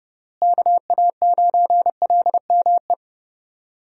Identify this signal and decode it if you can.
Morse code, heard as a keyed tone, about 30 wpm, 700 Hz. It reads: KA9LME
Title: USA Call Signs - 6 Characters - Repeated 3x - 30wpm